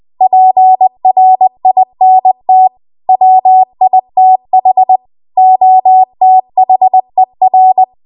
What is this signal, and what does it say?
Morse code, heard as a keyed tone, about 20 wpm, 750 Hz. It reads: PRINT WITH OTHER
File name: Archive / 20 WPM / 20WPM